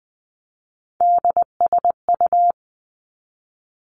Morse code, heard as a keyed tone, about 20 wpm, 700 Hz. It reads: DSU
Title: Three Challenging Letter-Number Combinations 20wpm